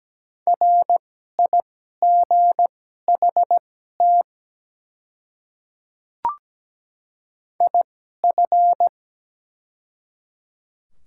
Morse code, heard as a keyed tone, about 17 wpm, spaced as for 12 wpm, 700 Hz. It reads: RIGHT IF
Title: 2x Character Spacing - Top 200 Words - 17wpm